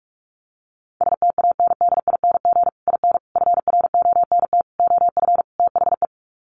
Morse code, heard as a keyed tone, about 44 wpm, 700 Hz. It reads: STANDING IN FRONT OF THE
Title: Sets of 5 Words Encore2 45wpm